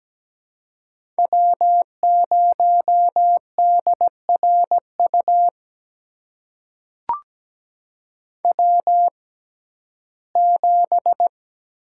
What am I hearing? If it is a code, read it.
Morse code, heard as a keyed tone, about 17 wpm, spaced as for 12 wpm, 700 Hz. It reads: W0DRU W7
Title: USA Call Signs - 5 Characters - 6x Character Spacing - 17wpm